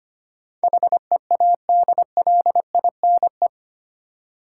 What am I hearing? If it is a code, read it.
Morse code, heard as a keyed tone, about 25 wpm, 700 Hz. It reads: HEADLINE
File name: Eight Letter Words - 25wpm